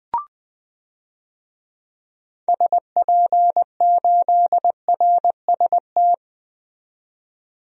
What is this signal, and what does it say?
Morse code, heard as a keyed tone, about 20 wpm, 700 Hz. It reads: SP8RST
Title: DXCC-100 Call Signs - 6 Characters - 20wpm